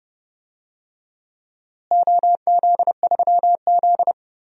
Morse code, heard as a keyed tone, about 31 wpm, 700 Hz. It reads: OZ3Z
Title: DXCC-100 Call Signs - 4 Characters - Repeated 2x - 30wpm